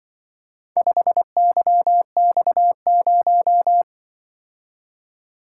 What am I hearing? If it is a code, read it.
Morse code, heard as a keyed tone, about 25 wpm, 700 Hz. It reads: 5YX0
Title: Speed-Racing - Four Letter-Number Combinations - 20wpm